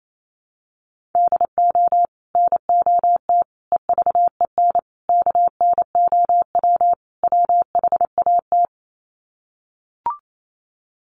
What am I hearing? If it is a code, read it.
Morse code, heard as a keyed tone, about 28 wpm, 700 Hz. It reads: DO NOT EVEN KNOW WHAT
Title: Sets of 5 Words Encore 28wpm